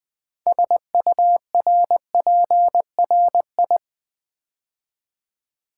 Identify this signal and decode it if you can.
Morse code, heard as a keyed tone, about 20 wpm, 700 Hz. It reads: SURPRI
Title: Build A Word - 2 Letters At A Time - 8-Letter Words - 20wpm